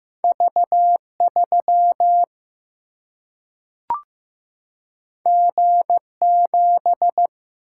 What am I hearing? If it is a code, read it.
Morse code, heard as a keyed tone, about 15 wpm, 700 Hz. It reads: V3 G7
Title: Single-Letter Single-Number 15wpm